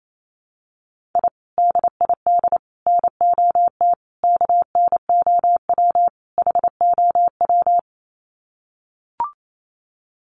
Morse code, heard as a keyed tone, about 28 wpm, 700 Hz. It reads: I DID NOT KNOW HOW
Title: Sets of 5 Words Encore 28wpm